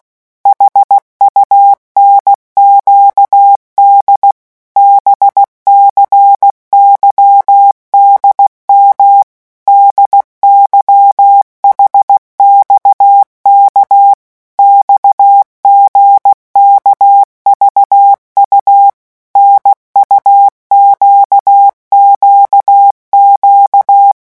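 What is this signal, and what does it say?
Morse code, heard as a keyed tone, about 16 wpm, 800 Hz. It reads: HUNQD BCYDM DYHXK XGKVU NUQQQ
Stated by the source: Morse letter-group revision drill at 15 words per minute.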